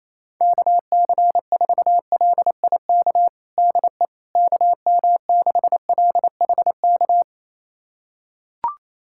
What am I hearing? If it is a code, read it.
Morse code, heard as a keyed tone, about 28 wpm, 700 Hz. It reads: KC4LIK DE KM6LHK
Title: USA Callsign DE Callsign - 6 Characters - 28wpm